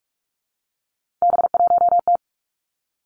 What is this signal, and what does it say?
Morse code, heard as a keyed tone, about 46 wpm, 700 Hz. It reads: B1T
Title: Three Challenging Letter-Number Combinations 45wpm